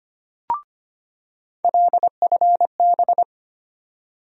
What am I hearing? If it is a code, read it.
Morse code, heard as a keyed tone, about 25 wpm, 700 Hz. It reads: LFB
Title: USA Call Sign Postfixes - 3 Letters 25wpm